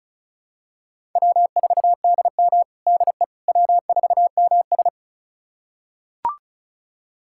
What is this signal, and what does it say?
Morse code, heard as a keyed tone, about 36 wpm, 700 Hz. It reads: W4DM DE W4MS
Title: USA Callsign DE Callsign - 4 Characters - 35wpm